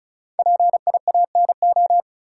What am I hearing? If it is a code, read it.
Morse code, heard as a keyed tone, about 36 wpm, 700 Hz. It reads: PIANO
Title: Five Letter Words - 35wpm